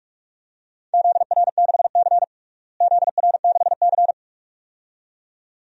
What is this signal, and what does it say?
Morse code, heard as a keyed tone, about 46 wpm, 700 Hz. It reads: ZR6C ZR6C
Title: DXCC-100 Call Signs - 4 Characters - Repeated 2x - 45wpm